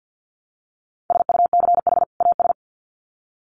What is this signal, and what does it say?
Morse code, heard as a keyed tone, about 50 wpm, 700 Hz. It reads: SUCH AS
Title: Sets of 2 Words 50wpm